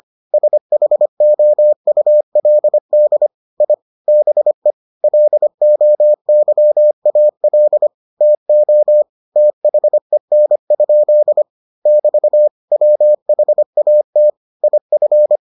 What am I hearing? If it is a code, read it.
Morse code, heard as a keyed tone, about 25 wpm, 600 Hz. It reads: SHOULD I BE LOYAL TO THEN? <BT> WHAT IF